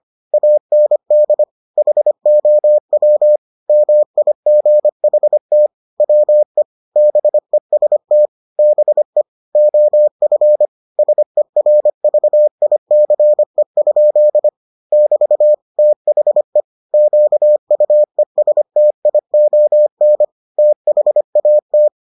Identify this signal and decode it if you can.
Morse code, heard as a keyed tone, about 25 wpm, 600 Hz. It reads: AND HOW MIGHT WE BEST BE OF SERVICE? <BT> THE QUESTION THAT